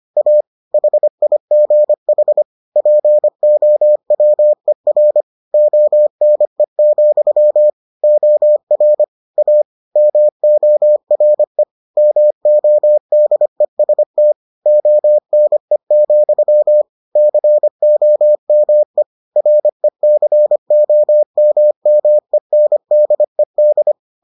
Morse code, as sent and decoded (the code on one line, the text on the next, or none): .- / .... .. --. .... / .--. --- .-- . .-. / --- -. --..-- / --- .-. / .- / -- --- .-. / -- --- -.. ... - / --- -. --..-- / -.-. --- -- . / .-. -.-. --- -- -- . -. -.. -..
A HIGH POWER ONE, OR A MORE MODEST ONE, COME RECOMMENDED